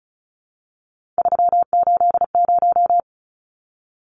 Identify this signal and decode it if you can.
Morse code, heard as a keyed tone, about 34 wpm, 700 Hz. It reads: 380
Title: Numbers - 3-Digit 35wpm